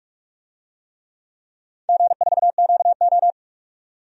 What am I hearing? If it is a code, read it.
Morse code, heard as a keyed tone, about 45 wpm, 700 Hz. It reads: G4QO
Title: DXCC-100 Call Signs - 4 Characters - 45wpm